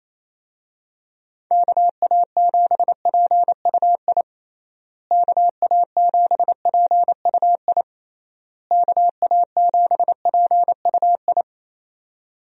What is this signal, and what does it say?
Morse code, heard as a keyed tone, about 28 wpm, 700 Hz. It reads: KA7PUI KA7PUI KA7PUI
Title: USA Call Signs - 6 Characters - Repeated 3x - 28wpm